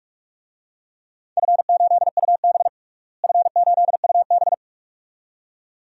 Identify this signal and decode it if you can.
Morse code, heard as a keyed tone, about 44 wpm, 700 Hz. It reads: F8UB F8UB
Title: DXCC-100 Call Signs - 4 Characters - Repeated 2x - 45wpm